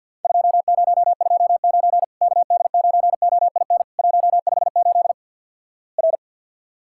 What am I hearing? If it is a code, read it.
Morse code, heard as a keyed tone, about 48 wpm, 700 Hz. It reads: 2029 KD9OIN 158 R
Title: Worked All Europe DX Contest - QTC Exchange - 50wpm